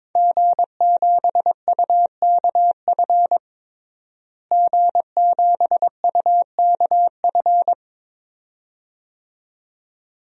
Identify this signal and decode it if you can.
Morse code, heard as a keyed tone, about 22 wpm, 700 Hz. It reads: G7UKF G7UKF
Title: DXCC-100 Call Signs - 5 Characters - Repeated 2x - 22wpm